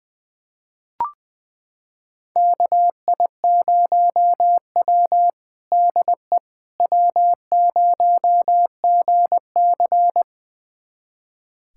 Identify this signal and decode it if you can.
Morse code, heard as a keyed tone, about 21 wpm, 700 Hz. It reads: KI0W DE W0GC
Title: USA Callsign DE Callsign - 4 Characters - 20wpm